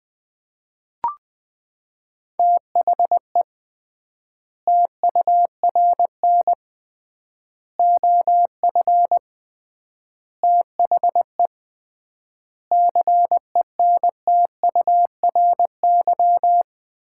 Morse code, heard as a keyed tone, about 20 wpm, 700 Hz. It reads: THE TURN OF THE CENTURY
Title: 3x Word-Spacing - Sets of 5 Words - 20wpm